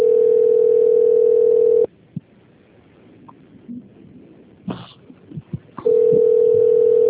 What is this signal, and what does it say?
Ringback tone (North America), 440 + 480 Hz.